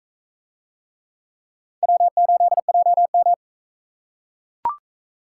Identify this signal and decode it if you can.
Morse code, heard as a keyed tone, about 41 wpm, 700 Hz. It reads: W8JM